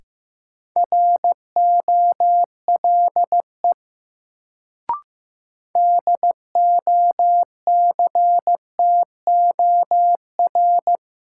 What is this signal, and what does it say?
Morse code, heard as a keyed tone, about 15 wpm, 700 Hz. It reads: ROLE DOCTOR
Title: Sentences from Top 1500 Words - New Word Drill Alt - 15wpm